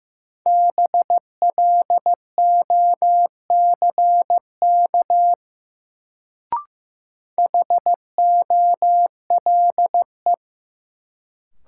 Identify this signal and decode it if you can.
Morse code, heard as a keyed tone, about 15 wpm, 700 Hz. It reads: BLOCK HOLE
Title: Sentences from Top 1600 Words - New Word Drill - 15wpm